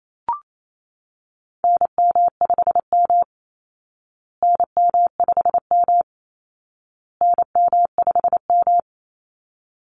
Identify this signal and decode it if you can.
Morse code, heard as a keyed tone, about 28 wpm, 700 Hz. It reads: NM5M NM5M NM5M
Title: Call Signs Repeated 2x - 28wpm